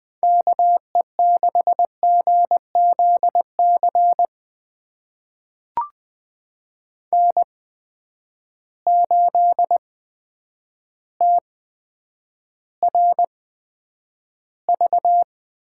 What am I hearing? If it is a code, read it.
Morse code, heard as a keyed tone, about 20 wpm, spaced as for 9 wpm, 700 Hz. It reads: KE6GZC N8TRV
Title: USA Call Signs - 8x Character Spacing - 20wpm